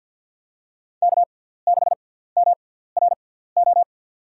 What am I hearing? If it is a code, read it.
Morse code, heard as a keyed tone, about 53 wpm, spaced as for 17 wpm, 700 Hz. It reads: K6MRO
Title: USA Call Signs - 5 Characters - 6x Character Spacing - 50wpm